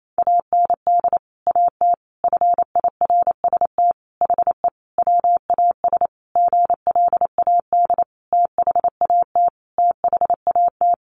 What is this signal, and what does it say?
Morse code, heard as a keyed tone, about 28 wpm, 700 Hz. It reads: AND AT FIRST HE WAS GLAD THAT THAT
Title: Sentences from Top 1300 Words 28wpm